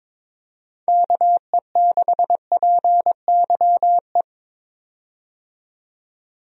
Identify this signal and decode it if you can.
Morse code, heard as a keyed tone, about 21 wpm, 700 Hz. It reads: KE6PYE
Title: USA Call Signs - 6 Characters - 22wpm